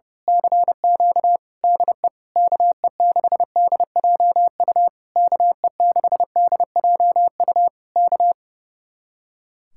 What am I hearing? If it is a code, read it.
Morse code, heard as a keyed tone, about 28 wpm, 700 Hz. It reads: CQ DE KE6DJU KE6DJU K
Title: Speed-Racing Common QSO Phrases - 30wpm